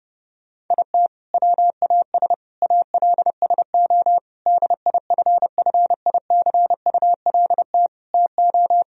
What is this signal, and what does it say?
Morse code, heard as a keyed tone, about 32 wpm, 700 Hz. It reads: IT WAS ALSO DIFFICULT TO
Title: Sentences from Top 1000 Words - Review Part 1 - 30wpm